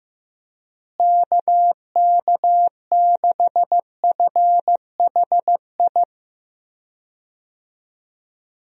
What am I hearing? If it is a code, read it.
Morse code, heard as a keyed tone, about 15 wpm, 700 Hz. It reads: KK6FHI
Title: USA Call Signs - 6 Characters - Repeated 2x - 15wpm